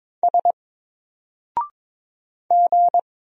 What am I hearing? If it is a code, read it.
Morse code, heard as a keyed tone, about 22 wpm, 700 Hz. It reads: S G
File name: Single-Letter-Number-22wpm